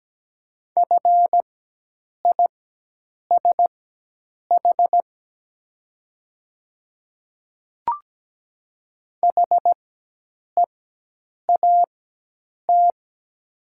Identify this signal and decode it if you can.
Morse code, heard as a keyed tone, about 17 wpm, spaced as for 8 wpm, 690 Hz. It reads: FISH HEAT
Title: 4x Character Spacing - Top 500 Words - 17wpm